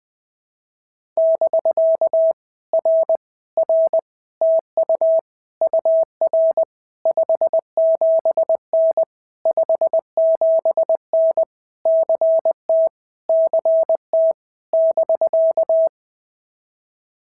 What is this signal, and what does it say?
Morse code, heard as a keyed tone, about 20 wpm, 650 Hz. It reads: <BK> R R TU UR 57N 57N CT CT <BK>
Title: POTA QSO Practice - 20wpm